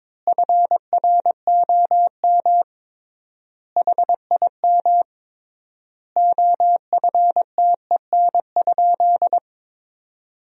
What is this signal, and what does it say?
Morse code, heard as a keyed tone, about 22 wpm, 700 Hz. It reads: FROM HIM OFTEN?
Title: 3x Word-Spacing - Sentences from Top 400 Words - 22wpm